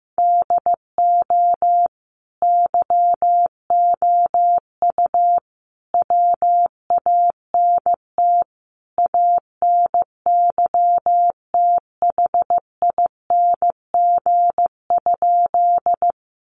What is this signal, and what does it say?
Morse code, heard as a keyed tone, about 15 wpm, 700 Hz. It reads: DO YOU WANT ANYTHING?
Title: Sentences from Top 1000 Words 15wpm